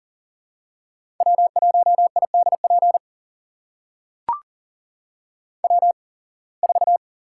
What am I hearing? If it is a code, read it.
Morse code, heard as a keyed tone, about 38 wpm, spaced as for 25 wpm, 700 Hz. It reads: W1IDP W4
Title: USA Call Signs - 5 Characters - 8x Character Spacing - 40wpm